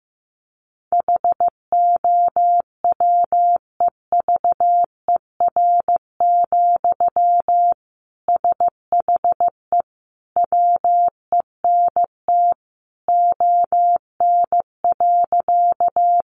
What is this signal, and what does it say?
Morse code, heard as a keyed tone, about 15 wpm, 700 Hz. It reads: HOWEVER, SHE WENT ON.
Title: Sentences from Top 600 Words 15wpm